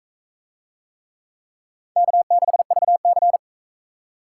Morse code, heard as a keyed tone, about 42 wpm, 700 Hz. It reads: K6VC